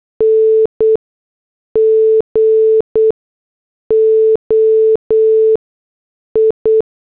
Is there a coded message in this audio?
Yes: Morse code from 0.2 to 6.8 s, heard as a keyed tone, about 8 wpm, spaced as for 6 wpm, 440 Hz. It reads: NGOI